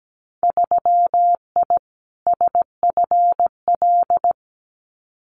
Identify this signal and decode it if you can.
Morse code, heard as a keyed tone, about 17 wpm, 700 Hz. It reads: 3I SFL
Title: Winter Field Day Exchanges 17wpm